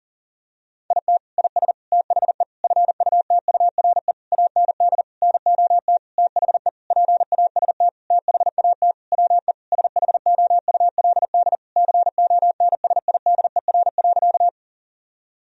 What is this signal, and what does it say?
Morse code, heard as a keyed tone, about 40 wpm, 700 Hz. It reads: IT IS THE FUTURE AND NOT THE PAST THAT WE SHOULD CONSIDER.
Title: Sentences from Top 1000 Words - Review Part 4 - 40wpm